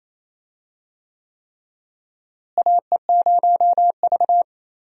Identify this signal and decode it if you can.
Morse code, heard as a keyed tone, about 28 wpm, 700 Hz. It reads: AE0V
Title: USA Call Signs - 4 Characters - Repeated 3x - 28wpm